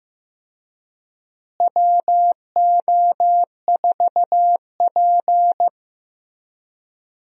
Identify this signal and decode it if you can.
Morse code, heard as a keyed tone, about 16 wpm, 700 Hz. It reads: WO4P
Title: USA Call Signs - 4 Characters - Repeated 3x - 15wpm